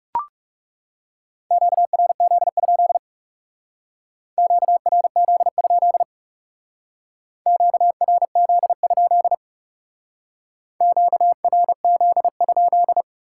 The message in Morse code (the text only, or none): QRZ? QRZ? QRZ? QRZ?